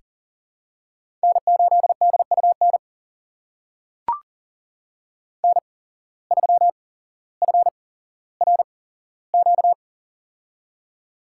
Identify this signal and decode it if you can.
Morse code, heard as a keyed tone, about 40 wpm, spaced as for 17 wpm, 700 Hz. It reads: N8DUN N3FRQ
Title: USA Call Signs - 5 Characters - 8x Character Spacing - 40wpm